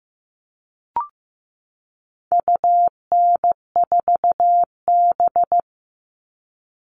Morse code, heard as a keyed tone, about 15 wpm, 700 Hz. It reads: UN4B
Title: All Cut Numbers - 4-Digit 15wpm